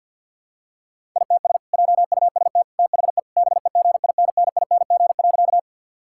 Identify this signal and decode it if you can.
Morse code, heard as a keyed tone, about 50 wpm, 700 Hz. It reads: ITS JUST THE BEGINNING.
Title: Sentences from Top 1000 Words - Review Part 5 - 50wpm